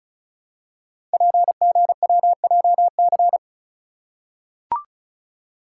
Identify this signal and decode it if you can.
Morse code, heard as a keyed tone, about 36 wpm, 690 Hz. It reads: PGWJC
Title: Code Groups - 5 Letters - 35wpm